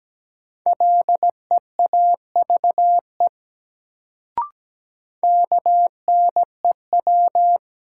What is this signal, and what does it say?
Morse code, heard as a keyed tone, about 17 wpm, 700 Hz. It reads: LEAVE KNEW
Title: Sentences from Top 400 Words - New Word Drill - 17wpm